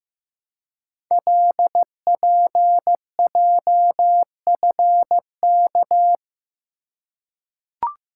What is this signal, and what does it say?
Morse code, heard as a keyed tone, about 15 wpm, 700 Hz. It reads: LPJFK
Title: Code Groups - 5 Letters - 15wpm